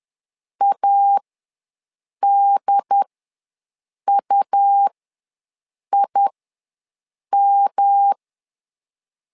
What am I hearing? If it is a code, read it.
Morse code, heard as a keyed tone, about 10 wpm, spaced as for 6 wpm, 800 Hz. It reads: ADUIM
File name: LEZIONI / 3 MWRO / 3 M ws 10wpm 2121